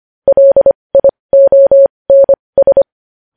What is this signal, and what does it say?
Morse code, heard as a keyed tone, about 25 wpm, spaced as for 20 wpm, 550 Hz. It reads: LIONS
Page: news headlines in morse code at 20 wpm